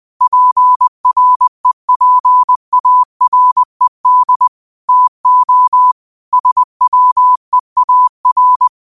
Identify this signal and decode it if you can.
Morse code, heard as a keyed tone, about 20 wpm, 1000 Hz. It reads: PREPARED TO SWEAR
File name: qotdfiles / 20wpm